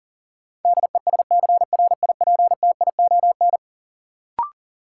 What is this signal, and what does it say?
Morse code, heard as a keyed tone, about 40 wpm, 700 Hz. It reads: DESCRIPTION
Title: Sentences from Top 1300 Words - New Word Drill - 40wpm